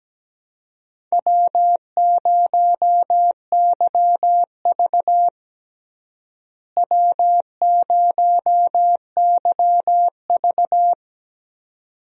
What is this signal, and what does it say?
Morse code, heard as a keyed tone, about 17 wpm, 700 Hz. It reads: W0YV W0YV
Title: USA Call Signs - 4 Characters - Repeated 2x - 17wpm